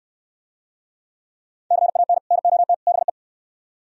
Morse code, heard as a keyed tone, about 69 wpm, 700 Hz. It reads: CAN NOT BE